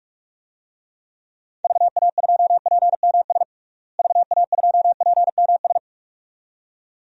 Morse code, heard as a keyed tone, about 41 wpm, 700 Hz. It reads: VA2PMS VA2PMS
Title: DXCC-100 Call Signs - 6 Characters - Repeated 2x - 45wpm